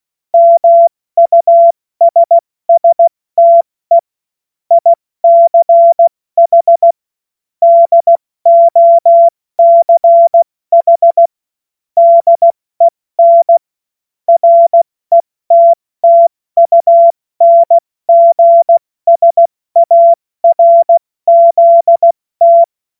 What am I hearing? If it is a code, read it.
Morse code, heard as a keyed tone, about 16 wpm, 680 Hz. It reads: MUSSTE ICH DOCH DEN RETTUNGSARZT
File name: Witz-09-September-2025-70-F80